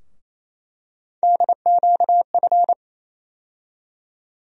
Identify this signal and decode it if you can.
Morse code, heard as a keyed tone, about 28 wpm, 700 Hz. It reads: DQF
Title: USA Call Sign Postfixes - 3 Letters 28wpm